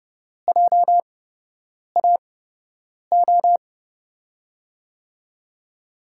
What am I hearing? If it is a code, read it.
Morse code, heard as a keyed tone, about 30 wpm, spaced as for 8 wpm, 700 Hz. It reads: JAO
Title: USA Call Signs - 6 Characters - 8x Character Spacing - 30wpm